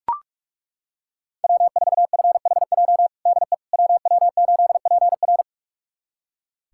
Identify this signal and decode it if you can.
Morse code, heard as a keyed tone, about 42 wpm, 700 Hz. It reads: W4FHJ DE WW8PR
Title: USA Callsign DE Callsign - 5 Characters - 45wpm